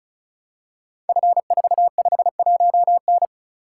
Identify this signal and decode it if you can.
Morse code, heard as a keyed tone, about 36 wpm, 690 Hz. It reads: F451N
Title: Code Groups - 5 Letter:Numbers - 35wpm